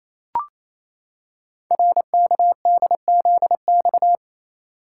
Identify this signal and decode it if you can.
Morse code, heard as a keyed tone, about 27 wpm, 700 Hz. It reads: RKDZX
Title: Code Groups - 5 Letters - 28wpm